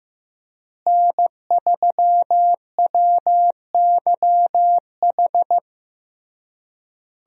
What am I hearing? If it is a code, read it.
Morse code, heard as a keyed tone, about 15 wpm, 700 Hz. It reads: N3WYH
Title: USA Call Signs - 5 Characters - Repeated 3x - 15wpm